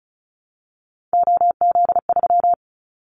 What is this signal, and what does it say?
Morse code, heard as a keyed tone, about 36 wpm, 700 Hz. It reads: OZ3
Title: Callsign Prefixes for Top 100 DXCC - 35wpm